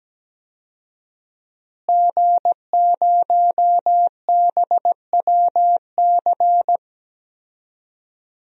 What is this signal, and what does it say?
Morse code, heard as a keyed tone, about 17 wpm, 700 Hz. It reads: G0BWC
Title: DXCC-100 Call Signs - 5 Characters - Repeated 3x - 17wpm